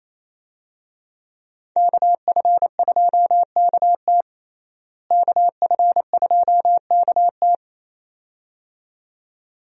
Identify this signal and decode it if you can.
Morse code, heard as a keyed tone, about 28 wpm, 700 Hz. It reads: KF2KT KF2KT
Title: USA Call Signs - 5 Characters - Repeated 2x - 28wpm